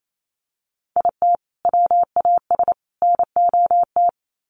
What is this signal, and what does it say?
Morse code, heard as a keyed tone, about 28 wpm, 700 Hz. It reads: IT WAS NOT
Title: Sets of 3 Words 28wpm